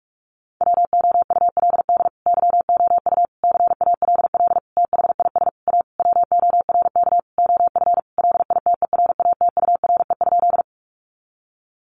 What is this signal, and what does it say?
Morse code, heard as a keyed tone, about 44 wpm, 700 Hz. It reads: WOULD YOU CALL THIS A WORK OF LITERATURE?
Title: Sentences from Top 600 Words 45wpm